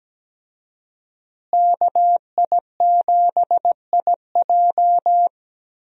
Morse code, heard as a keyed tone, about 17 wpm, 700 Hz. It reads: KI7IJ
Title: USA Call Signs - 5 Characters - Repeated 3x - 17wpm